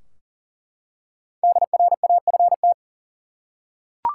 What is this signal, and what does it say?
Morse code, heard as a keyed tone, about 40 wpm, 700 Hz. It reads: DRAFT
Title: Five Letter Words - 40wpm